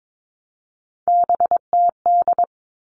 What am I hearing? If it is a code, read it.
Morse code, heard as a keyed tone, about 22 wpm, 700 Hz. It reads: BTD
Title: All Cut Numbers - 3-Digit 22wpm